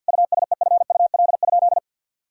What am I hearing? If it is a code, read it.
Morse code, heard as a keyed tone, about 50 wpm, 700 Hz. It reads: USEFUL?
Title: Sentences from Top 1000 Words - Review Part 4 - 50wpm